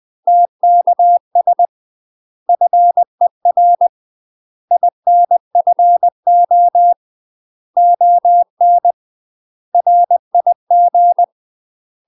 Morse code, heard as a keyed tone, about 20 wpm, 700 Hz. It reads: TKS FER INFO ON RIG
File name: UK MASTER QSOs TRAINING / 20WPM / UK MASTER QSOs 0018